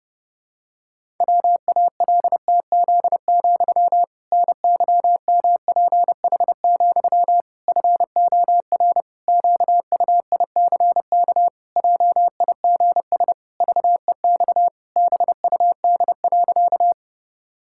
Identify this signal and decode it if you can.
Morse code, heard as a keyed tone, about 30 wpm, 700 Hz. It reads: WALTZ, NYMPH, FOR QUICK JIGS VEX BUD.
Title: Pangram Challenge v2 - 30wpm